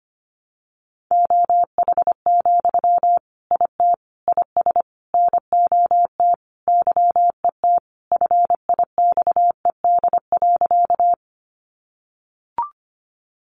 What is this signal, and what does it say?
Morse code, heard as a keyed tone, about 25 wpm, 700 Hz. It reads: OH, IT IS NOT YET FIXED.
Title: Sentences from Top 1100 Words 25wpm